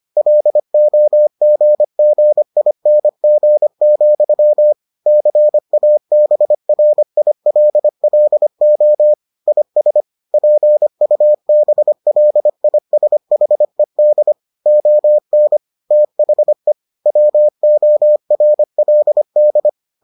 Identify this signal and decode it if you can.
Morse code, heard as a keyed tone, about 25 wpm, 600 Hz. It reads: LOGGING, CABRILLO IS PUBLISHED ON THE WORLD